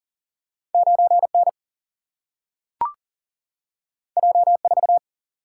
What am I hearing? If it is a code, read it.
Morse code, heard as a keyed tone, about 40 wpm, 700 Hz. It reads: 9N J4
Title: Single-Letter Single-Number 40wpm